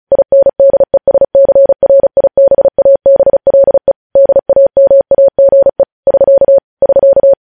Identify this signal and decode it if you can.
Morse code, heard as a keyed tone, about 36 wpm, 550 Hz. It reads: INDESCRIBABLE DAMAGE <SK> <SK>
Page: News Headlines in Morse Code at 35 WPM